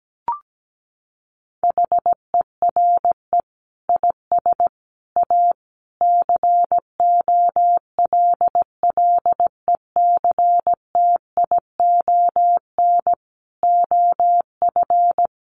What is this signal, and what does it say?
Morse code, heard as a keyed tone, about 17 wpm, 700 Hz. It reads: HERE IS A COLLECTION OF
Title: Sentences from Top 1200 Words 17wpm